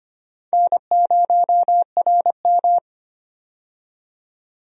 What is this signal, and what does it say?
Morse code, heard as a keyed tone, about 25 wpm, 700 Hz. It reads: N0RM
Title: USA Call Signs - 4 Characters - 25wpm